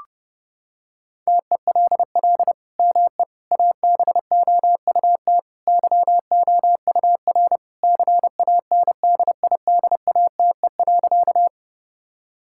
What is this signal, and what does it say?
Morse code, heard as a keyed tone, about 30 wpm, 700 Hz. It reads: TELL ME ABOUT YOUR CANDIDATE.
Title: Sentences from Top 1600 Words 30wpm